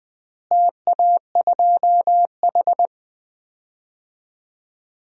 Morse code, heard as a keyed tone, about 20 wpm, 700 Hz. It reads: TA2H
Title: DXCC-100 Call Signs - 4 Characters - Repeated 2x - 20wpm